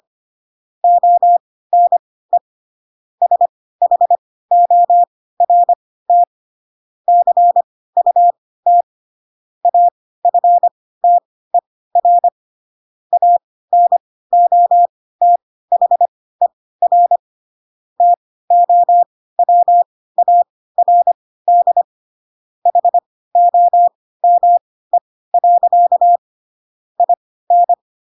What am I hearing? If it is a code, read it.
Morse code, heard as a keyed tone, about 25 wpm, spaced as for 16 wpm, 700 Hz. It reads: ONE SHORT CUT AFTER ANOTHER TOWARD HOME. IN
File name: Pinocchio / Pinocchio ch4 10 25 ik1hge